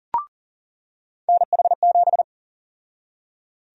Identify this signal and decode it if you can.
Morse code, heard as a keyed tone, about 39 wpm, 700 Hz. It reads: NH7
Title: USA Call Sign Prefixes - 40wpm